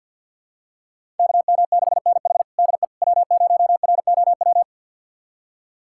Morse code, heard as a keyed tone, about 51 wpm, 700 Hz. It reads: KM6NH DE W0ROW
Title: USA Callsign DE Callsign - 5 Characters - 50wpm